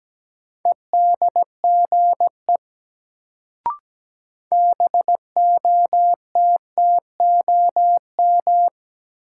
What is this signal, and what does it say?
Morse code, heard as a keyed tone, about 17 wpm, 700 Hz. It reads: EDGE BOTTOM
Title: Sets of 3 Words - New Word Drill - 17wpm